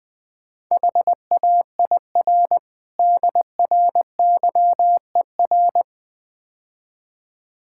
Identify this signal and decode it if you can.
Morse code, heard as a keyed tone, about 20 wpm, 700 Hz. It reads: HAIR DRYER
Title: Easy Words - Speed-Racing with Hints - 20wpm